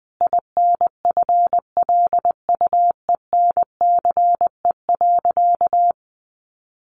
Morse code, heard as a keyed tone, about 20 wpm, 700 Hz. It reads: INFLUENCE.
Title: Sentences from Top 1100 Words 20wpm